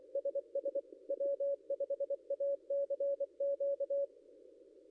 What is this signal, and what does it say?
Morse code, heard as a keyed tone, about 24 wpm, 560 Hz. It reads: SS W5ACQ
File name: contests / ss-cw-k5tr-op-n5rz -0800